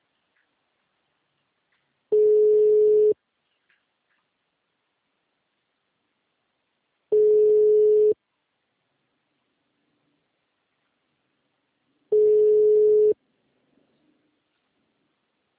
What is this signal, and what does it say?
Ringback tone (Europe, 425 Hz).
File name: out- -2028- - - .0